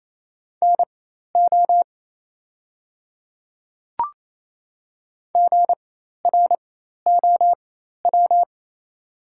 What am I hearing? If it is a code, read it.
Morse code, heard as a keyed tone, about 28 wpm, spaced as for 13 wpm, 690 Hz. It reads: NO GROW
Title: 4x Character Spacing - Top 500 Words - 28wpm